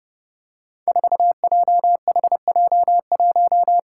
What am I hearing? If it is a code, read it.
Morse code, heard as a keyed tone, about 32 wpm, 700 Hz. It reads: 4JHJ1
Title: Five Challenging Letter-Number Combinations 30wpm